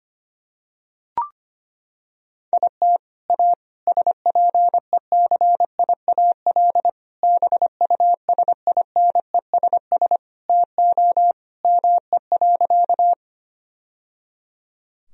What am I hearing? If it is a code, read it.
Morse code, heard as a keyed tone, about 25 wpm, 700 Hz. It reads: IT A SPECIAL BUSINESS TO ME.
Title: Sentences from Top 1000 Words - Review Part 3 - 25wpm